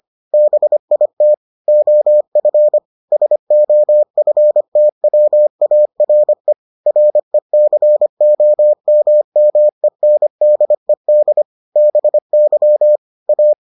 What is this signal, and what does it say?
Morse code, heard as a keyed tone, about 25 wpm, 600 Hz. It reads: BIT OF SOFTWARE RECOMMENDED BY A